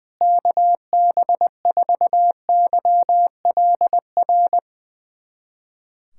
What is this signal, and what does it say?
Morse code, heard as a keyed tone, about 20 wpm, 700 Hz. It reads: KB4YLR
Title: USA Callsign DE Callsign - 6 Characters - 20wpm